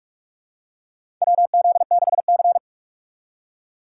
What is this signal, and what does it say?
Morse code, heard as a keyed tone, about 47 wpm, 700 Hz. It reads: WZ6C
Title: USA Call Signs - 3x Character Spacing - 45wpm